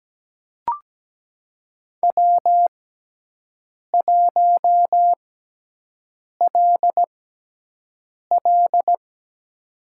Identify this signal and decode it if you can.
Morse code, heard as a keyed tone, about 17 wpm, spaced as for 6 wpm, 700 Hz. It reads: W1LL
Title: USA Call Signs - 6x Character Spacing - 17wpm